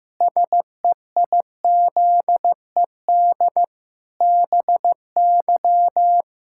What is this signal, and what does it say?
Morse code, heard as a keyed tone, about 15 wpm, 700 Hz. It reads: SEIZED BY
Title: Pangram Challenge v2 - 15wpm